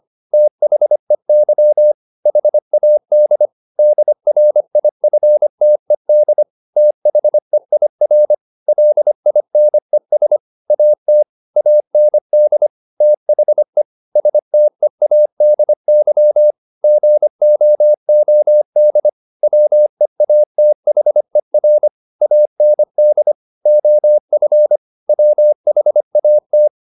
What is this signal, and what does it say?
Morse code, heard as a keyed tone, about 25 wpm, 600 Hz. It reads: THEY HAD DRIFTED THEIR LINES AT AND THE STEADY GOOD WEATHER AND OF WHAT
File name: the old man and the sea part 1 25wpm